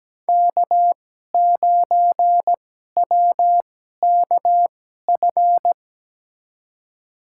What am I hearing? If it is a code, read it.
Morse code, heard as a keyed tone, about 17 wpm, spaced as for 12 wpm, 700 Hz. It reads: K9WKF
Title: USA Call Signs - 5 Characters - 2x Character Spacing - 17wpm